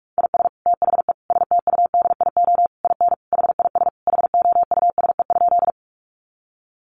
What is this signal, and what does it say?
Morse code, heard as a keyed tone, about 44 wpm, 700 Hz. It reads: IS THE STUDIO IN HIS HOUSE?
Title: Sentences from Top 1200 Words 45wpm